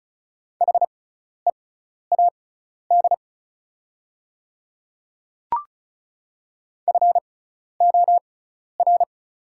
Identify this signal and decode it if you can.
Morse code, heard as a keyed tone, about 36 wpm, spaced as for 12 wpm, 690 Hz. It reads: HEAD FOR
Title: 6x Character Spacing - Top 500 Words - 35wpm